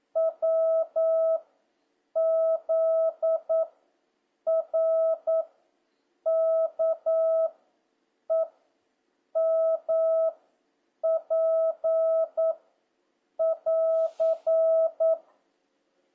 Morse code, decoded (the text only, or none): W Z R K E M P <AR>